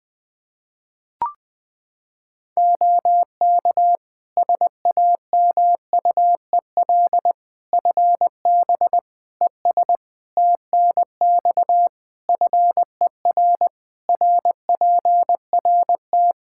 Morse code, heard as a keyed tone, about 21 wpm, 700 Hz. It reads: OK SAMUEL FB ES TNX FER RPRT
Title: Practice QSO - Exchange 2 - Info - 20wpm